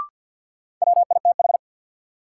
Morse code, heard as a keyed tone, about 53 wpm, 700 Hz. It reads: WITH
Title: Sets of 2 Words - New Word Drill - 50wpm